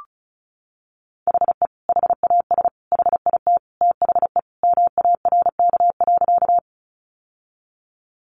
Morse code, heard as a keyed tone, about 34 wpm, 700 Hz. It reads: HE HAS HIT THE MARK.
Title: Sentences from Top 1200 Words 35wpm